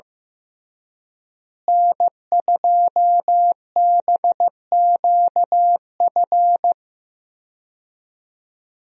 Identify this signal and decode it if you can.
Morse code, heard as a keyed tone, about 15 wpm, 700 Hz. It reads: N2BQF
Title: USA Call Signs - 5 Characters - Repeated 3x - 15wpm